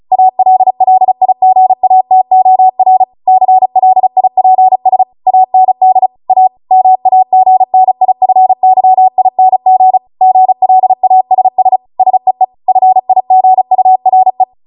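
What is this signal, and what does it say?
Morse code, heard as a keyed tone, about 36 wpm, 750 Hz. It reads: ALLIGATOR CLIPS AND A MAGNIFYING GLASS SEE FIGURE
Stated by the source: Amateur radio Morse code practice archive transmission at 35 wpm.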